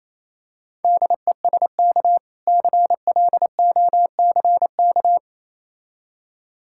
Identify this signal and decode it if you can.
Morse code, heard as a keyed tone, about 27 wpm, 690 Hz. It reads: DESK CLOCK
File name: Easy Words - Speed-Racing with Hints - 22wpm